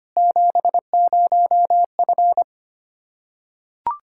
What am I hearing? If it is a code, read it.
Morse code, heard as a keyed tone, about 25 wpm, 700 Hz. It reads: 70F
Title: Practice QSO - Exchange 2 - Info - 25wpm